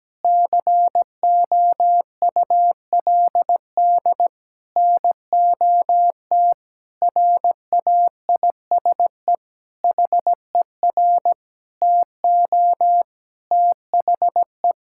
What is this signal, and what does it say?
Morse code, heard as a keyed tone, about 17 wpm, 700 Hz. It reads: COULD NOT RAISE HER TO THE
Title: Sentences from Top 1600 Words 17wpm